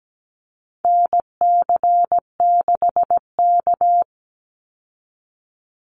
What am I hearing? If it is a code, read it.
Morse code, heard as a keyed tone, about 17 wpm, 700 Hz. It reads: NC6K
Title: Call Signs - 17wpm